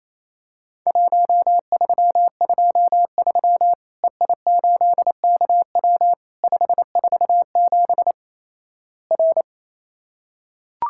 Morse code, heard as a keyed tone, about 28 wpm, 700 Hz. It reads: 1323 EI8KW 547 R
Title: Worked All Europe DX Contest - QTC Exchange - 28wpm